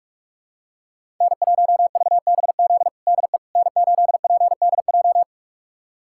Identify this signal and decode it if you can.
Morse code, heard as a keyed tone, about 46 wpm, 700 Hz. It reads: N1VBZ DE N8PDJ
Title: USA Callsign DE Callsign - 5 Characters - 45wpm